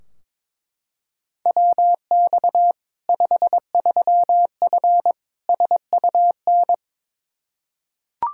Morse code, heard as a keyed tone, about 21 wpm, 700 Hz. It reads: WX 53F SUN
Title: Practice QSO - Exchange 2 - Info - 22wpm